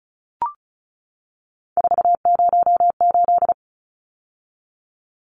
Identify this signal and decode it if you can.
Morse code, heard as a keyed tone, about 34 wpm, 700 Hz. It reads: 408
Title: Numbers - 3-Digit 35wpm